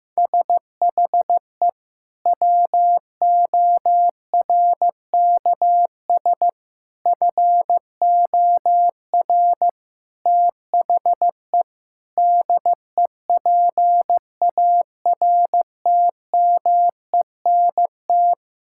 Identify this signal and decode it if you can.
Morse code, heard as a keyed tone, about 15 wpm, 700 Hz. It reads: SHE WORKS FOR THE DEPARTMENT
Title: Sentences from Top 1000 Words - Review Part 5 - 15wpm